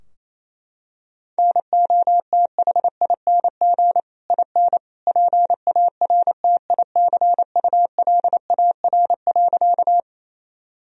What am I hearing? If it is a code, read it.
Morse code, heard as a keyed tone, about 28 wpm, 700 Hz. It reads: NOTHING IN PARTICULAR.
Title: Sentences from Top 1000 Words - Review Part 4 - 28wpm